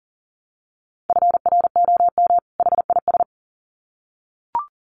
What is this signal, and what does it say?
Morse code, heard as a keyed tone, about 39 wpm, 700 Hz. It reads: FROM HIS
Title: Sets of 2 Words 40wpm